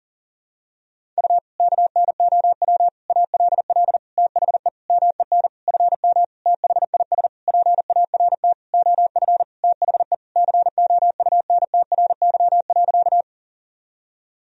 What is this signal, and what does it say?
Morse code, heard as a keyed tone, about 40 wpm, 700 Hz. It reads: U KNOW ALL THE MEN FM THIS PART OF THE COUNTRY.
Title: Sentences from Top 300 Words and Abbreviations 40wpm